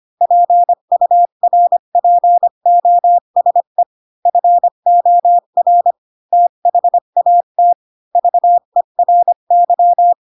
Morse code, heard as a keyed tone, about 25 wpm, 700 Hz. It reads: PURPOSE FOR THAT VERY
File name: Pinocchio / Pinocchio ch3 20 25 ik1hge